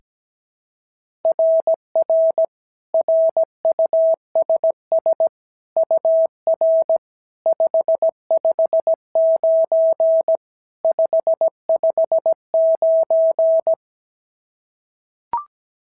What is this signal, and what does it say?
Morse code, heard as a keyed tone, about 16 wpm, 650 Hz. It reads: RR RUSS UR 559 559
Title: SOTA QSO Practice 17wpm